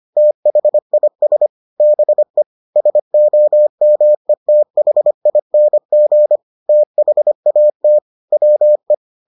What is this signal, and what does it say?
Morse code, heard as a keyed tone, about 25 wpm, 600 Hz. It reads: THIS BE SOMETHING THAT WE